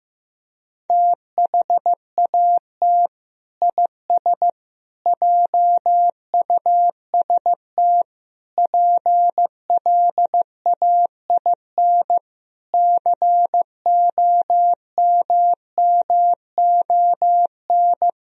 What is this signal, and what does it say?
Morse code, heard as a keyed tone, about 16 wpm, 700 Hz. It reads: THAT IS JUST PLAIN COMMON
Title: Sentences from Top 1000 Words - Review Part 4 - 15wpm